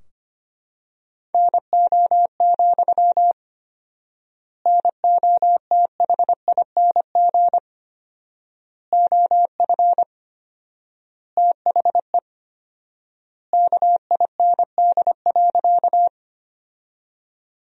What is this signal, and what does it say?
Morse code, heard as a keyed tone, about 25 wpm, 700 Hz. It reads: NO, NOTHING OF THE KIND.
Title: 4x Word-Spacing - Sentences from Top 500 Words - 25wpm